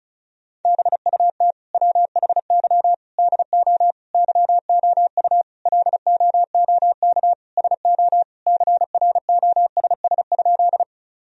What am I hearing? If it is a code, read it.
Morse code, heard as a keyed tone, about 34 wpm, 700 Hz. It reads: BUT WHY DO YOU LOOK SO CROSS?
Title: Sentences from Top 1000 Words - Review Part 1 - 35wpm